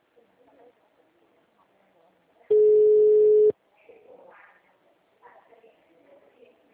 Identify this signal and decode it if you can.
Ringback tone (Europe, 425 Hz), with speech.